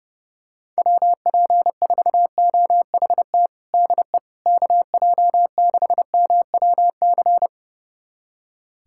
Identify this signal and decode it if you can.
Morse code, heard as a keyed tone, about 30 wpm, 700 Hz. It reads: WP4OHT DE KJ6MWC
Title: USA Callsign DE Callsign 30wpm